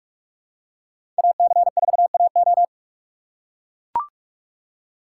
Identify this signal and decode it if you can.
Morse code, heard as a keyed tone, about 42 wpm, 700 Hz. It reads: AC4AO
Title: USA Call Signs - 5 Characters - 8x Character Spacing - 45wpm